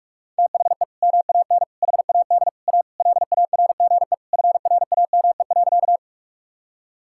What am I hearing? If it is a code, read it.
Morse code, heard as a keyed tone, about 46 wpm, 700 Hz. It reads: THE MAN HAD A LARGE FRAME.
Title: Sentences from Top 1000 Words - Review Part 6 - 45wpm